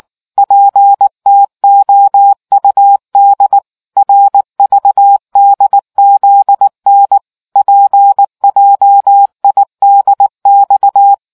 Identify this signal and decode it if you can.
Morse code, heard as a keyed tone, about 19 wpm, 800 Hz. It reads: PTOUD RVDZN PJIDX